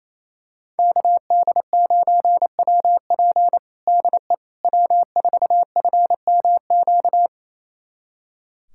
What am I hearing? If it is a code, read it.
Morse code, heard as a keyed tone, about 28 wpm, 700 Hz. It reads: KD9WP DE W4FMQ
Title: USA Callsign DE Callsign - 5 Characters - 28wpm